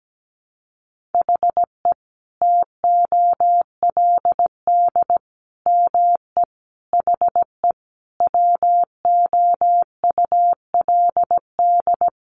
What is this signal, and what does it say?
Morse code, heard as a keyed tone, about 17 wpm, 700 Hz. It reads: HE TOLD ME HE WOULD
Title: Sentences from Top 1300 Words 17wpm